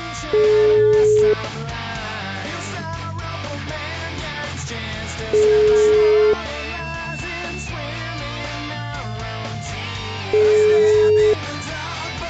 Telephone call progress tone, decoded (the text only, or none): ringback tone